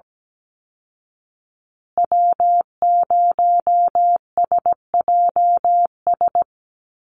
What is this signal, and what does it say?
Morse code, heard as a keyed tone, about 17 wpm, 700 Hz. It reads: W0SJS
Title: Call Signs Repeated 3x - 17wpm